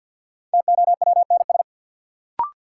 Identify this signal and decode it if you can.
Morse code, heard as a keyed tone, about 48 wpm, 700 Hz. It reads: TOWNS
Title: Sentences from Top 1000 Words - Review Part 6 - New Word Drill - 50wpm